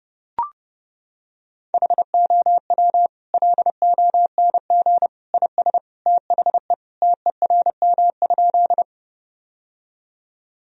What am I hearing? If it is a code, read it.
Morse code, heard as a keyed tone, about 30 wpm, 700 Hz. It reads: HOW LONG IS THE TERM?
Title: Sentences from Top 1000 Words - Review Part 4 - 30wpm